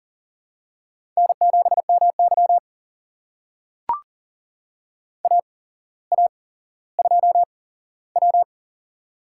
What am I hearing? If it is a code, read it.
Morse code, heard as a keyed tone, about 40 wpm, spaced as for 17 wpm, 700 Hz. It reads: N7MY AA2W
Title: USA Call Signs - 4 Characters - 8x Character Spacing - 40wpm